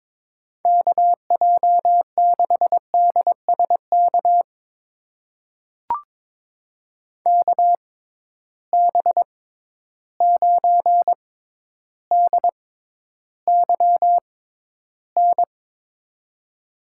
Morse code, heard as a keyed tone, about 22 wpm, spaced as for 11 wpm, 700 Hz. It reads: KJ6DSK KB9DYN
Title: USA Call Signs - 6 Characters - 6x Character Spacing - 22wpm